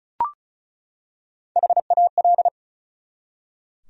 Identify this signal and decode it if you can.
Morse code, heard as a keyed tone, about 35 wpm, 700 Hz. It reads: HAL
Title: USA Common Names - Max 5 Letters - 35wpm